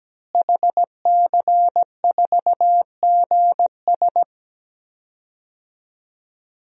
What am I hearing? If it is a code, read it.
Morse code, heard as a keyed tone, about 17 wpm, 700 Hz. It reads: HC4GS
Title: DXCC-100 Call Signs - 5 Characters - Repeated 3x - 17wpm